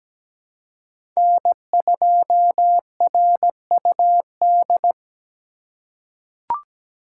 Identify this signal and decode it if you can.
Morse code, heard as a keyed tone, about 17 wpm, 700 Hz. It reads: N2RUD
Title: USA Call Signs - 3x Character Spacing - 17wpm